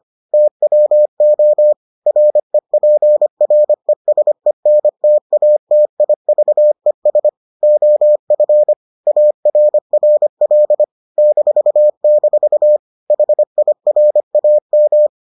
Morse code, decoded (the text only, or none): TWO REPRESENTATIVES OF ARRL -- HIRAM